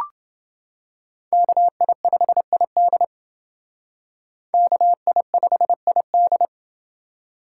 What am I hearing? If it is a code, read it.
Morse code, heard as a keyed tone, about 29 wpm, 700 Hz. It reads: KI5ID KI5ID